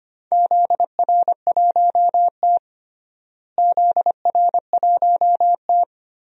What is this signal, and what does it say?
Morse code, heard as a keyed tone, about 25 wpm, 700 Hz. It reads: ZR1T ZR1T
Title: DXCC-100 Call Signs - 4 Characters - Repeated 3x - 25wpm